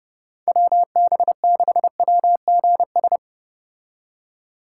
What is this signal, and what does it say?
Morse code, heard as a keyed tone, about 31 wpm, 700 Hz. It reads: WB6WGS